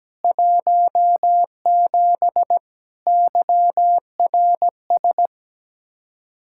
Morse code, heard as a keyed tone, about 17 wpm, 700 Hz. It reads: 17 YRS
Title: Speed-Racing Common QSO Phrases - 17wpm